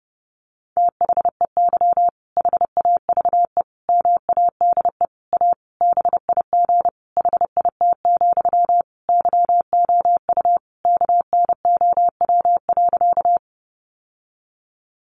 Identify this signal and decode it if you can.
Morse code, heard as a keyed tone, about 30 wpm, 700 Hz. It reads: THEY HAVE MADE A BIG HIT, YOU KNOW.
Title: Sentences from Top 1200 Words 30wpm